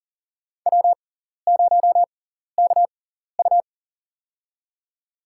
Morse code, heard as a keyed tone, about 40 wpm, spaced as for 14 wpm, 700 Hz. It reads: W0KU
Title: USA Call Signs - 4 Characters - 6x Character Spacing - 40wpm